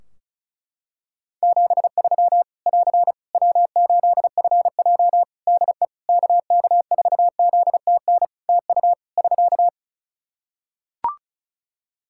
Morse code, heard as a keyed tone, about 34 wpm, 700 Hz. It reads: 73 <AR> W8FJ DE KK4ZTN TU <SK>
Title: Practice QSO - Exchange 4 - Ending - 35wpm